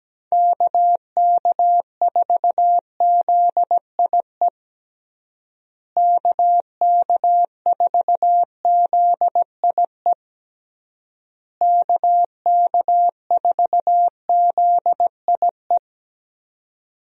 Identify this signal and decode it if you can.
Morse code, heard as a keyed tone, about 17 wpm, 700 Hz. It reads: KK4ZIE KK4ZIE KK4ZIE
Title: USA Call Signs - 6 Characters - Repeated 3x - 17wpm